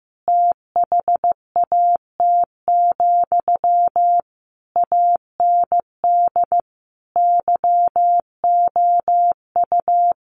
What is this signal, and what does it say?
Morse code, heard as a keyed tone, about 15 wpm, 700 Hz. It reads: THAT, AND YOU
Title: Sentences from Top 1400 Words 15wpm